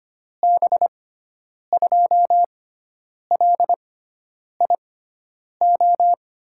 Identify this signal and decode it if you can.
Morse code, heard as a keyed tone, about 25 wpm, spaced as for 9 wpm, 700 Hz. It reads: B2LIO
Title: USA Call Signs - 6 Characters - 6x Character Spacing - 25wpm